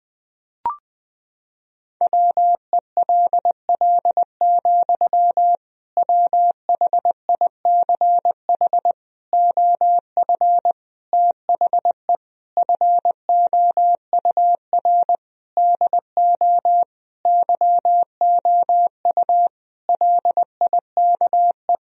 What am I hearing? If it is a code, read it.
Morse code, heard as a keyed tone, about 20 wpm, 700 Hz. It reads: WELL, WHICH OF THE FOUR DO YOU LIKE
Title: Sentences from Top 1000 Words - Review Part 1 - 20wpm